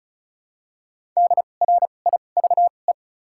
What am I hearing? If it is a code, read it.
Morse code, heard as a keyed tone, about 36 wpm, spaced as for 25 wpm, 700 Hz. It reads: DRIVE
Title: 2x Character Spacing - Top 500 Words - 35wpm